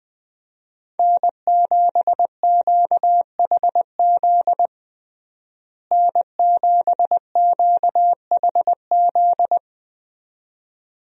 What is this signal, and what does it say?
Morse code, heard as a keyed tone, about 20 wpm, 700 Hz. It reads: N7QHZ N7QHZ
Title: USA Call Signs - 5 Characters - Repeated 2x - 20wpm